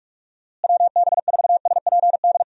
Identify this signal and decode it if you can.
Morse code, heard as a keyed tone, about 46 wpm, 700 Hz. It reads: WB4SPD
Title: USA Call Signs - 6 Characters - Repeated 2x - 45wpm